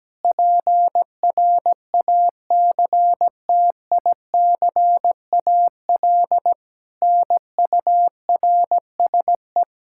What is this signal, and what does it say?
Morse code, heard as a keyed tone, about 17 wpm, 700 Hz. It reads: PRACTICAL NURSE
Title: Practice QSO - Exchange 3 - More Info - 17wpm